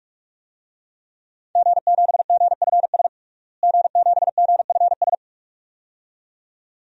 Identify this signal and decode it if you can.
Morse code, heard as a keyed tone, about 41 wpm, 700 Hz. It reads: G7GFS G7GFS
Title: DXCC-100 Call Signs - 5 Characters - Repeated 2x - 45wpm